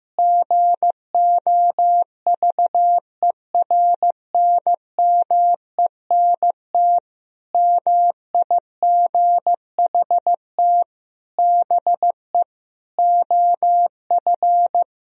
Morse code, heard as a keyed tone, about 16 wpm, 700 Hz. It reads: GOVERNMENT MIGHT BE OF
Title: Sentences from Top 1000 Words - Review Part 3 - 15wpm